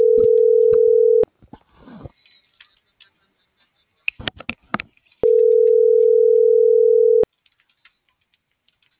Ringback tone (North America), 440 + 480 Hz.